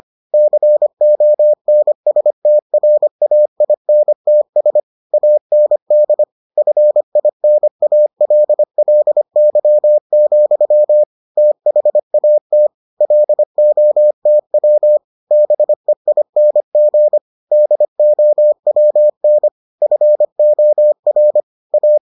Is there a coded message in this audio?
Yes: Morse code from 0.3 to 22.0 s, heard as a keyed tone, about 25 wpm, 600 Hz. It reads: CONSTRAINTS AND FINALLY, THAT LOTW BEING DOWN FOR A